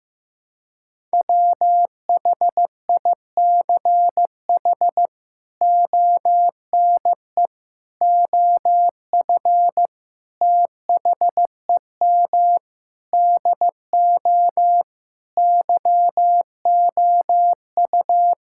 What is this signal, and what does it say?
Morse code, heard as a keyed tone, about 16 wpm, 700 Hz. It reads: WHICH ONE OF THEM DO YOU
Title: Sentences from Top 1000 Words - Review Part 5 - 15wpm